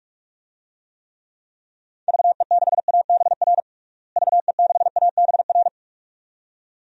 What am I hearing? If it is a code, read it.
Morse code, heard as a keyed tone, about 47 wpm, 700 Hz. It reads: VE6ABR VE6ABR
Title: DXCC-100 Call Signs - 6 Characters - Repeated 2x - 45wpm